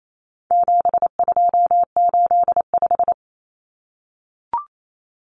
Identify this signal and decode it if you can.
Morse code, heard as a keyed tone, about 28 wpm, 700 Hz. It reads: 7285
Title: Common Cut Numbers - 4-Digit 28wpm